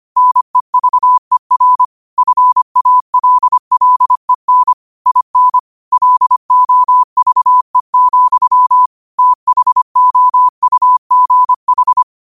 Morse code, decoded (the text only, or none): NEVER FALLEN IN LOVE, THOUGH